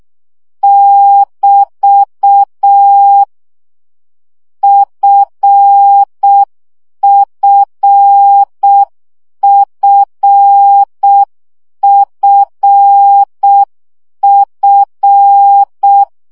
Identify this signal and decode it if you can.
Morse code, heard as a keyed tone, about 6 wpm, 800 Hz. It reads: <BT> FFFFF
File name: CW gyakorlas f 5 baud